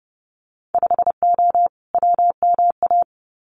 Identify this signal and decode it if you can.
Morse code, heard as a keyed tone, about 30 wpm, 700 Hz. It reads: 5O WMA
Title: Winter Field Day Exchanges Encore 30wpm